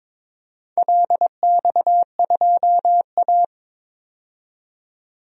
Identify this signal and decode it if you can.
Morse code, heard as a keyed tone, about 22 wpm, 700 Hz. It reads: LX2A
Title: DXCC-100 Call Signs - 4 Characters - 22wpm